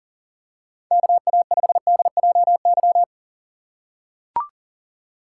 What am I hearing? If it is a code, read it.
Morse code, heard as a keyed tone, about 40 wpm, 700 Hz. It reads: KA5DJY
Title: USA Call Signs - 6 Characters - 6x Character Spacing - 40wpm